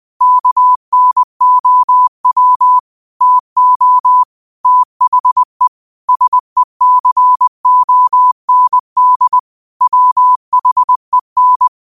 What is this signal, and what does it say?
Morse code, heard as a keyed tone, about 20 wpm, 1000 Hz. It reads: KNOW TO THE SECOND WHEN
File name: qotdfiles / 20wpm